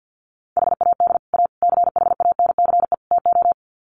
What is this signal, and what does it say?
Morse code, heard as a keyed tone, about 50 wpm, 700 Hz. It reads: HAD A CHANCE TO
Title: Sets of 4 Words 50wpm